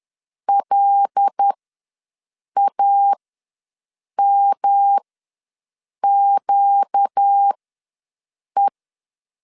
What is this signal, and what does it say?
Morse code, heard as a keyed tone, about 10 wpm, spaced as for 6 wpm, 800 Hz. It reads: LAMQE
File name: LEZIONI / 6 PKX0 / 6 PK ws 10wpm 3745